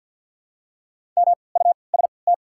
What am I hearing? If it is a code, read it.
Morse code, heard as a keyed tone, about 48 wpm, spaced as for 28 wpm, 700 Hz. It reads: MUST
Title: 3x Character Spacing - Top 200 Words - 50wpm